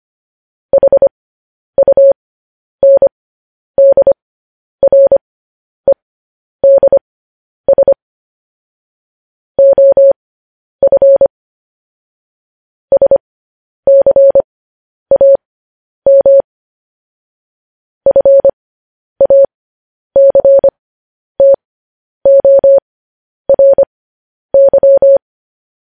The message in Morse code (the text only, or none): HUNDREDS OF SCAM FACTORY